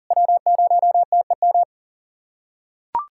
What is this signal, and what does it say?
Morse code, heard as a keyed tone, about 41 wpm, 690 Hz. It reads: W0TEM
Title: SOTA QSO Practice 40wpm